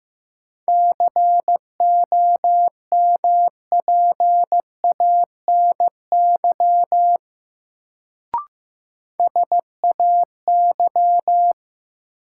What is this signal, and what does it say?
Morse code, heard as a keyed tone, about 15 wpm, 700 Hz. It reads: COMPANY SAY
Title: Sets of 2 Words - New Word Drill - 15wpm